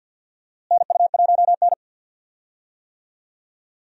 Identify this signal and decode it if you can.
Morse code, heard as a keyed tone, about 50 wpm, 700 Hz. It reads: NU1N
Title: USA Call Signs - 4 Characters - Repeated 3x - 50wpm